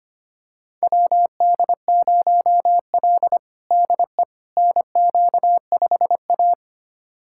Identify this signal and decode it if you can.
Morse code, heard as a keyed tone, about 25 wpm, 700 Hz. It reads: WD0L DE NQ5A
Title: USA Callsign DE Callsign - 4 Characters - 25wpm